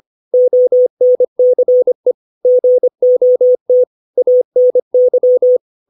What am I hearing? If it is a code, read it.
Morse code, heard as a keyed tone, about 25 wpm, 500 Hz. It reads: ONCE GOT ANY
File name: 632 words / 632 Words 25WPM